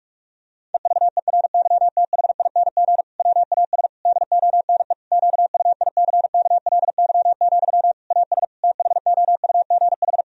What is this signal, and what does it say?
Morse code, heard as a keyed tone, about 46 wpm, 700 Hz. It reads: EVERYTHING WAS DONE QUICKLY, AS THOUGH
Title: Sentences from Top 1000 Words - Review Part 2 - 45wpm